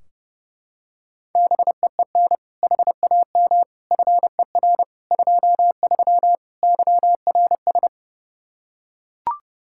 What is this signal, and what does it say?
Morse code, heard as a keyed tone, about 31 wpm, 700 Hz. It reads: BEEN HAM FER 23 YRS
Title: Practice QSO - Exchange 3 - More Info - 30wpm